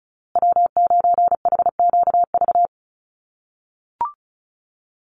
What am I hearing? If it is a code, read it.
Morse code, heard as a keyed tone, about 34 wpm, 700 Hz. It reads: W9HQV
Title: USA Call Signs Repeated 2x 35wpm